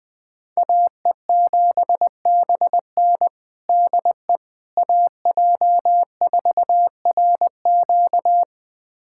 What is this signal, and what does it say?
Morse code, heard as a keyed tone, about 20 wpm, 700 Hz. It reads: AE7BN DE AJ4RQ
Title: USA Callsign DE Callsign - 5 Characters - 20wpm